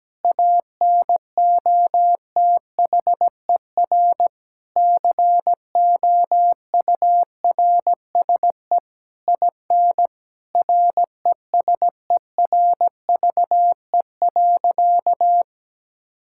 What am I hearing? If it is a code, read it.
Morse code, heard as a keyed tone, about 17 wpm, 700 Hz. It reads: ANOTHER COURSE IN RESERVE.
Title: Sentences from Top 1000 Words - Review Part 6 - 17wpm